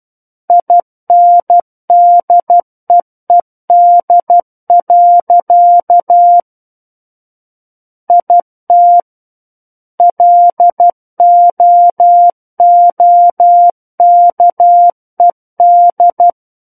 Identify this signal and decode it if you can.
Morse code, heard as a keyed tone, about 12 wpm, 700 Hz. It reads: INDEED. IT LOOKED